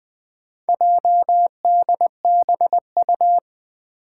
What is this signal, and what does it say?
Morse code, heard as a keyed tone, about 20 wpm, 700 Hz. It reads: JDBU
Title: Four Challenging Letter-Number Combinations 20wpm